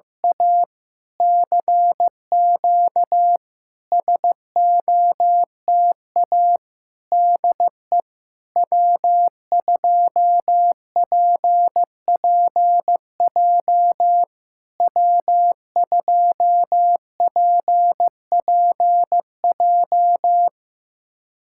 Morse code, heard as a keyed tone, about 16 wpm, 700 Hz. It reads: A CQ SOTA DE W2PPJ W2PPJ
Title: SOTA QSO Practice 15wpm